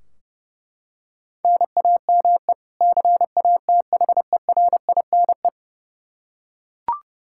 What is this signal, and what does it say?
Morse code, heard as a keyed tone, about 30 wpm, 700 Hz. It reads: NAME CATHERINE
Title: Practice QSO - Exchange 1 - Beginning - 30wpm